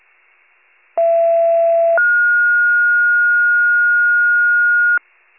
Two-tone page, heard as a one-second tone, then a three-second tone, about 670 then 1430 Hz. It is heard from 1.0 to 5.0 s.